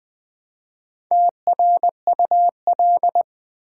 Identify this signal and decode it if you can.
Morse code, heard as a keyed tone, about 20 wpm, 700 Hz. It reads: TRUL
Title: Build A Word - 1 Letter At A Time - 5-Letter Words - 20wpm